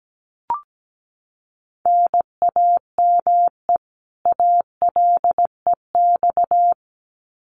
Morse code, heard as a keyed tone, about 17 wpm, 700 Hz. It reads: NAME ALEX
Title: Common QSO Phrases 17wpm